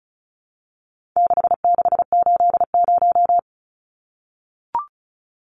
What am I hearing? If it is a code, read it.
Morse code, heard as a keyed tone, about 34 wpm, 700 Hz. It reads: 6680
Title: Numbers - 4-Digit 35wpm